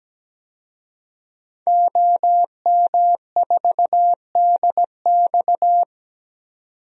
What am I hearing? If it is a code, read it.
Morse code, heard as a keyed tone, about 17 wpm, 700 Hz. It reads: OM4DX
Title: DXCC-100 Call Signs - 5 Characters - 17wpm